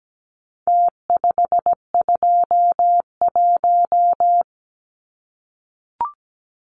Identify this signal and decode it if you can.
Morse code, heard as a keyed tone, about 17 wpm, 700 Hz. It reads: T521
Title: Numbers - Contest Style - 4-Digit 17wpm